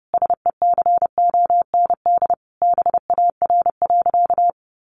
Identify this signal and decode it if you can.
Morse code, heard as a keyed tone, about 30 wpm, 700 Hz. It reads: SECOND BAR.
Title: Easy Phrases 30wpm